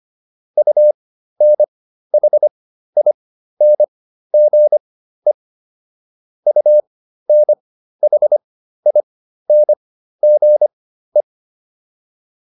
Morse code, heard as a keyed tone, about 25 wpm, spaced as for 13 wpm, 600 Hz. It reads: UNHINGE UNHINGE
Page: intermediate-practice-files